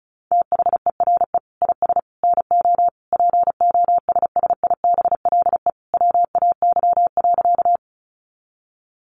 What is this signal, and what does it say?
Morse code, heard as a keyed tone, about 34 wpm, 700 Hz. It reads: THERE IS NO POSSIBLE WAY.
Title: Sentences from Top 500 Words 35wpm